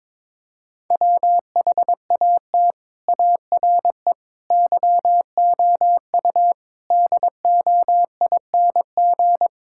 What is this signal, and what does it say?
Morse code, heard as a keyed tone, about 21 wpm, 700 Hz. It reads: WHAT ARE YOU DOING
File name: practice / 22wpm / Sentences with 4-Word Phrases - 22wpm